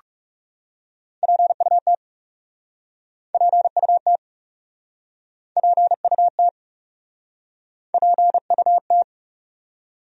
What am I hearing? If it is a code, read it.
Morse code, heard as a keyed tone, about 40 wpm, 700 Hz. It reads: PUT PUT PUT PUT